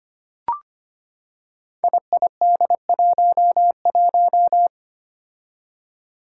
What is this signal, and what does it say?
Morse code, heard as a keyed tone, about 25 wpm, 700 Hz. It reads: IID11
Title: Five Challenging Letter-Number Combinations 25wpm